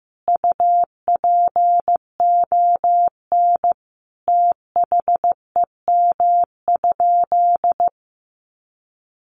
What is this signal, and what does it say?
Morse code, heard as a keyed tone, about 15 wpm, 700 Hz. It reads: UPON THEM?
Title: Sentences from Top 1300 Words 15wpm